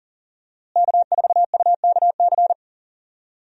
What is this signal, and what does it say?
Morse code, heard as a keyed tone, about 37 wpm, 700 Hz. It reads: K4UKC
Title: USA Call Signs - 5 Characters - Repeated 3x - 40wpm